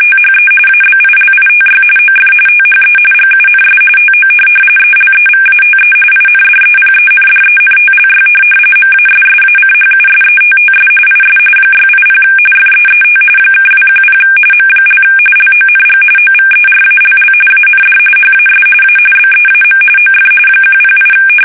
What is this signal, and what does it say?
RTTY, about 45.45 baud. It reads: THE ABILITY OF THE GUNS TO ELEVATE INDEPENDENTLY MASSACHUSETTS IS 680 FEET (210 M) LONG OVERALL AND HAS A BEAM OF 108
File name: fsk850 / USS-Massachusetts